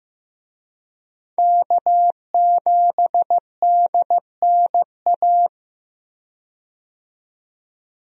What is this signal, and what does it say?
Morse code, heard as a keyed tone, about 15 wpm, 700 Hz. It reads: K7DNA
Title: USA Call Signs - 5 Characters - 15wpm